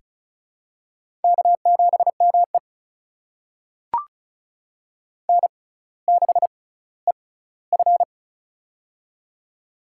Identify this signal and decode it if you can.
Morse code, heard as a keyed tone, about 36 wpm, spaced as for 18 wpm, 690 Hz. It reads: K7ME N6EF
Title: USA Call Signs - 4 Characters - 6x Character Spacing - 35wpm